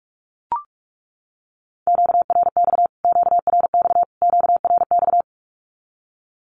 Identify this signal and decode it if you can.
Morse code, heard as a keyed tone, about 44 wpm, 700 Hz. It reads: QRX QRX QRX
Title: Q-Codes Repeated 3x 45wpm